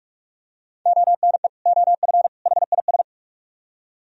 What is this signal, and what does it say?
Morse code, heard as a keyed tone, about 47 wpm, 700 Hz. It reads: ONE OF HIS